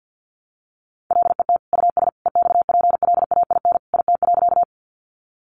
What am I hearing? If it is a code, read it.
Morse code, heard as a keyed tone, about 51 wpm, 700 Hz. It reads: LET US EXPLAIN IT.
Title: Sentences from Top 1100 Words 50wpm